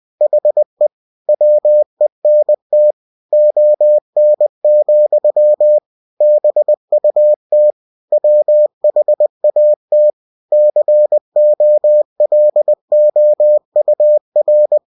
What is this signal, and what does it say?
Morse code, heard as a keyed tone, about 20 wpm, 600 Hz. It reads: HE WENT ON, BUT WHAT COLOUR
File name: cw-books / winnie-the-pooh / winnie-20wpm-page-0056